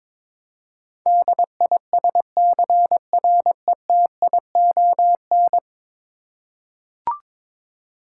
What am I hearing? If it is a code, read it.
Morse code, heard as a keyed tone, about 21 wpm, 700 Hz. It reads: DISCRETION
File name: practice / 22wpm / 10 Letter Words - First 3 Letters Given - 22wpm